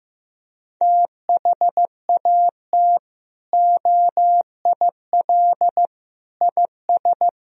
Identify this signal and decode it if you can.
Morse code, heard as a keyed tone, about 15 wpm, 700 Hz. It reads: THAT OIL IS
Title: Sentences from Top 1000 Words - Review Part 4 - 15wpm